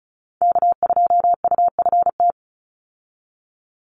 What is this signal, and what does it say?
Morse code, heard as a keyed tone, about 34 wpm, 700 Hz. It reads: K2UFT
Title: Call Signs - 35wpm